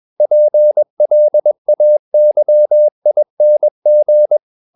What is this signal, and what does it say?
Morse code, heard as a keyed tone, about 21 wpm, 600 Hz. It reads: PLAYING
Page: CW med Gnister